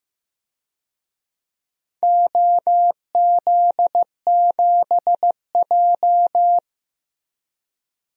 Morse code, heard as a keyed tone, about 16 wpm, 700 Hz. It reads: OZ7J
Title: DXCC-100 Call Signs - 4 Characters - Repeated 2x - 15wpm